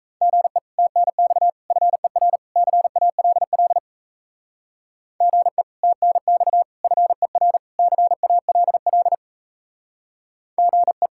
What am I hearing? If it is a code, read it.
Morse code, heard as a keyed tone, about 33 wpm, 700 Hz. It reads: GE TNX FER CALL GE TNX FER CALL GE
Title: Speed-Racing Common QSO Phrases - 28wpm